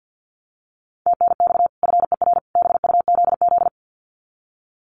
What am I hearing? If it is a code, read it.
Morse code, heard as a keyed tone, about 50 wpm, 700 Hz. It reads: TNX FER BUZZ
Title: Common QSO Phrases 50wpm